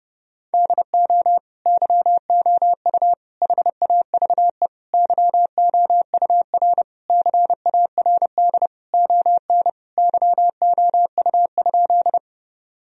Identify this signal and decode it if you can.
Morse code, heard as a keyed tone, about 30 wpm, 700 Hz. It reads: DO YOU HAVE YOUR CARD ON YOU?
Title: Sentences from Top 1000 Words - Review Part 4 - 30wpm